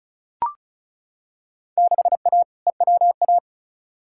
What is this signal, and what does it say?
Morse code, heard as a keyed tone, about 36 wpm, 690 Hz. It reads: 6A EWA
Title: Field Day Exchanges Encore - 35wpm